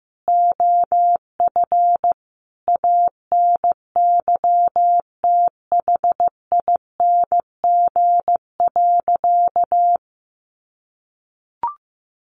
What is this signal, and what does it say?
Morse code, heard as a keyed tone, about 15 wpm, 700 Hz. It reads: OF ANYTHING.
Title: Sentences from Top 1200 Words 15wpm